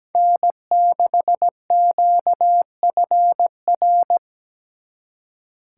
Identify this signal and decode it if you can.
Morse code, heard as a keyed tone, about 20 wpm, 700 Hz. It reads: N6QFR
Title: Speed Racing - USA 1x3 Call Signs 17wpm